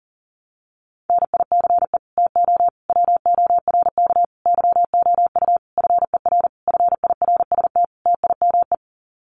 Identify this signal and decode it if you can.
Morse code, heard as a keyed tone, about 40 wpm, 700 Hz. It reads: NICE TO WORK YOU FER FIRST TIME
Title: Common QSO Phrases 40wpm